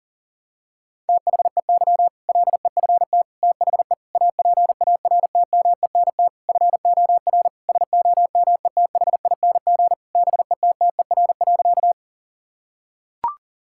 Morse code, heard as a keyed tone, about 40 wpm, 700 Hz. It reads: THEY LEFT THE APARTMENT FOR SOMETHING BETTER.
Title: Sentences from Top 1000 Words - Review Part 6 - 40wpm